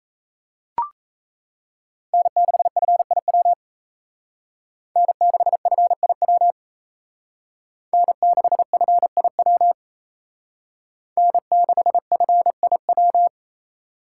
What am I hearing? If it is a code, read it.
Morse code, heard as a keyed tone, about 35 wpm, 700 Hz. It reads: N6FIW N6FIW N6FIW N6FIW
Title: Speed Racing - USA 1x3 Call Signs 28wpm